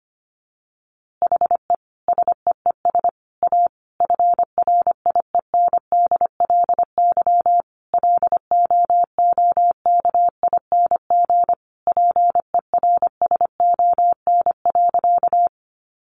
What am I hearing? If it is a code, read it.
Morse code, heard as a keyed tone, about 25 wpm, 700 Hz. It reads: HE SEES A FRIENDLY LOOKING PERSON.
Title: Easy Phrases 25wpm